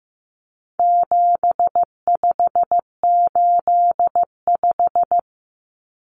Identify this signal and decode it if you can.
Morse code, heard as a keyed tone, about 15 wpm, 700 Hz. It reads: 7585
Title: Common Cut Numbers - 4-Digit 15wpm